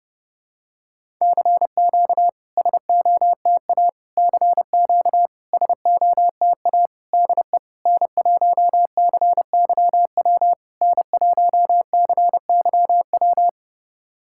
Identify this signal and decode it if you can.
Morse code, heard as a keyed tone, about 30 wpm, 690 Hz. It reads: CQ SOTA CQ SOTA DE N1CYW N1CYW
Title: SOTA QSO Practice 30wpm